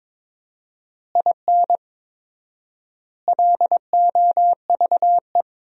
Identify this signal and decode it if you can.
Morse code, heard as a keyed tone, about 22 wpm, 700 Hz. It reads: IN LOVE
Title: 4x Word-Spacing - Sentences from Top 400 Words - 22wpm